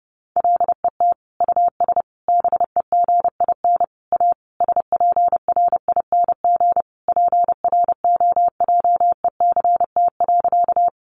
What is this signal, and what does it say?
Morse code, heard as a keyed tone, about 30 wpm, 700 Hz. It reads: LET US BEGIN A SPRING PROJECT.
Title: Sentences from Top 1400 Words 30wpm